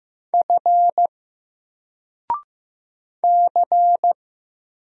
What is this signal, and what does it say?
Morse code, heard as a keyed tone, about 15 wpm, 700 Hz. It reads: F C
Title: Single Letter-Number 15wpm